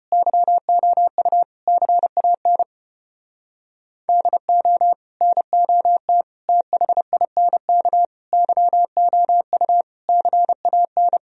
Morse code, heard as a keyed tone, about 33 wpm, 700 Hz. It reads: YOU CAN DO NOT THINK YOU CAN